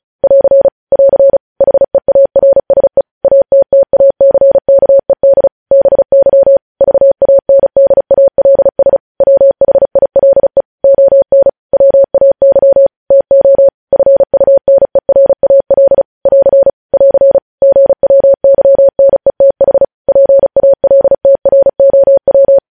Morse code, heard as a keyed tone, about 36 wpm, 550 Hz. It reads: <AR> <AR> HEARSE ATTACKED BY VANDALS WHILE ON WAY TO FUNERAL <AR> <AR> GWYNETH PALTROW
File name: podcast / News headlines-1x-35x35